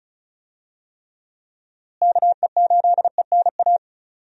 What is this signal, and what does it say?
Morse code, heard as a keyed tone, about 36 wpm, 700 Hz. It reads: KE8ENA
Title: USA Call Signs - 6 Characters - Repeated 3x - 35wpm